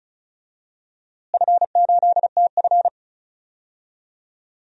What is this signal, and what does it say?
Morse code, heard as a keyed tone, about 36 wpm, 700 Hz. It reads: F8TF
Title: DXCC-100 Call Signs - 4 Characters - Repeated 3x - 35wpm